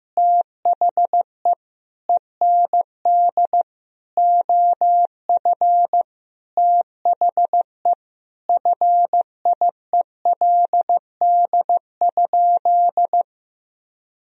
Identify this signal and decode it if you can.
Morse code, heard as a keyed tone, about 15 wpm, 700 Hz. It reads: THE END OF THE FIELD?
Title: Sentences from Top 1000 Words - Review Part 2 - 15wpm